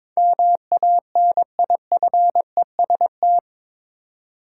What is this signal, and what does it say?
Morse code, heard as a keyed tone, about 21 wpm, 700 Hz. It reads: MANIFEST
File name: practice / 22wpm / Eight Letter Words - 22wpm